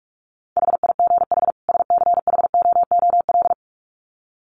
Morse code, heard as a keyed tone, about 44 wpm, 700 Hz. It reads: HIGH SCHOOL
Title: Sets of 2 Words 45wpm